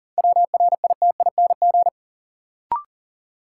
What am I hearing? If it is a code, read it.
Morse code, heard as a keyed tone, about 40 wpm, 700 Hz. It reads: WRITING
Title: Sentences from Top 800 Words - New Word Drill - 40wpm